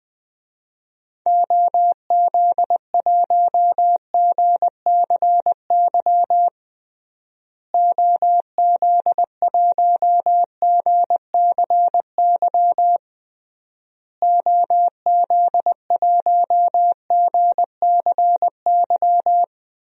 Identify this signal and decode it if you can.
Morse code, heard as a keyed tone, about 20 wpm, 700 Hz. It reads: OZ1GCY OZ1GCY OZ1GCY
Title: DXCC-100 Call Signs - 6 Characters - Repeated 3x - 20wpm